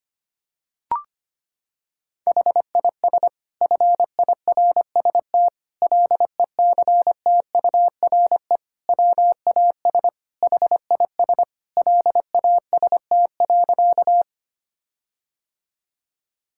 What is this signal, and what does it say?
Morse code, heard as a keyed tone, about 25 wpm, 700 Hz. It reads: HIS FIRST LECTURE WAS HIS LAST.
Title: Sentences from Top 1600 Words 25wpm